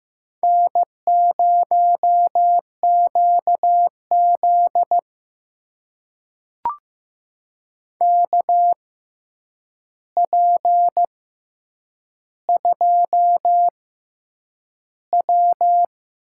Morse code, heard as a keyed tone, about 15 wpm, spaced as for 8 wpm, 700 Hz. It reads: N0QZ KP2W
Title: USA Call Signs - 4 Characters - 6x Character Spacing - 15wpm